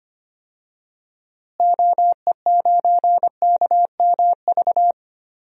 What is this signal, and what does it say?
Morse code, heard as a keyed tone, about 25 wpm, 700 Hz. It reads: OE9KMV
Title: DXCC-100 Call Signs - 6 Characters - Repeated 3x - 25wpm